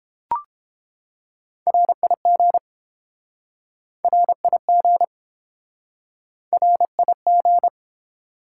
Morse code, heard as a keyed tone, about 30 wpm, 700 Hz. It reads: RIG RIG RIG